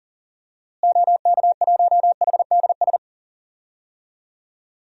Morse code, heard as a keyed tone, about 39 wpm, 700 Hz. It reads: OK1HDS
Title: DXCC-100 Call Signs - 6 Characters - Repeated 3x - 40wpm